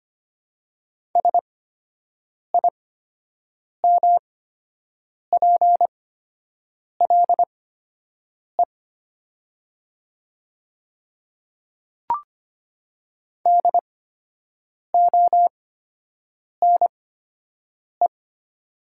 Morse code, heard as a keyed tone, about 25 wpm, spaced as for 7 wpm, 700 Hz. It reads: SIMPLE DONE
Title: 8x Character Spacing - Top 500 Words - 25wpm